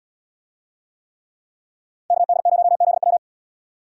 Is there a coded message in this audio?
Yes: Morse code from 2.1 to 3.2 s, heard as a keyed tone, about 77 wpm, 700 Hz. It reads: KD0ZW